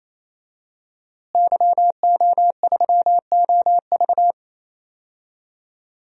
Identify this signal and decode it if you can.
Morse code, heard as a keyed tone, about 28 wpm, 700 Hz. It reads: YO3OV
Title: DXCC-100 Call Signs - 5 Characters - Repeated 3x - 28wpm